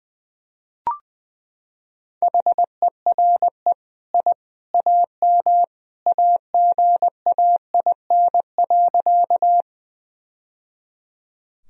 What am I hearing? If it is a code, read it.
Morse code, heard as a keyed tone, about 21 wpm, 700 Hz. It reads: HERE I AM AGAIN.
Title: Sentences from Top 1000 Words - Review Part 2 - 20wpm